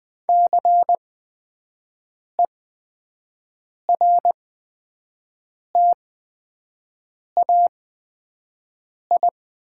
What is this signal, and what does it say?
Morse code, heard as a keyed tone, about 20 wpm, spaced as for 5 wpm, 700 Hz. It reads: CERTAI
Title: 8x Character Spacing - Top 500 Words - 20wpm